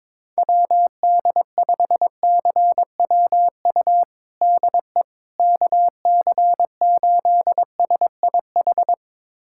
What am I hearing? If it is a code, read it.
Morse code, heard as a keyed tone, about 22 wpm, 700 Hz. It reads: WD5CWU DE KC8SIH
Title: USA Callsign DE Callsign 22wpm